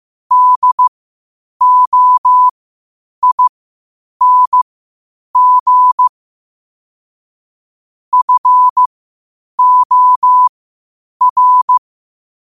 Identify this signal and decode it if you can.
Morse code, heard as a keyed tone, about 15 wpm, spaced as for 8 wpm, 1000 Hz. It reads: DOING FOR
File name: qotdfiles / 8wpm